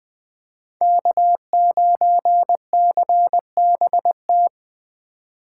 Morse code, heard as a keyed tone, about 20 wpm, 700 Hz. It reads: K9CBT